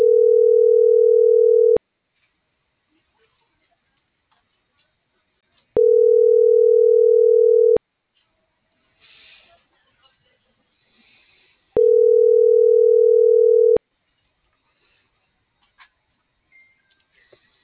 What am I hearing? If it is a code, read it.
Ringback tone (North America), 440 + 480 Hz.